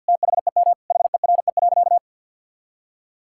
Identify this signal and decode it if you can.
Morse code, heard as a keyed tone, about 50 wpm, 700 Hz. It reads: THEM HERE.
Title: Sentences from Top 1000 Words - Review Part 5 - 50wpm